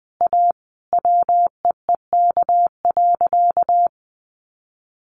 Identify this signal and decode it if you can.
Morse code, heard as a keyed tone, about 20 wpm, 700 Hz. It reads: A WEEK.
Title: Easy Phrases 20wpm